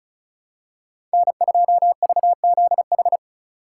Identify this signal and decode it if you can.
Morse code, heard as a keyed tone, about 36 wpm, 700 Hz. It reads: N2VZH
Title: USA Call Signs - 5 Characters - 2x Character Spacing - 35wpm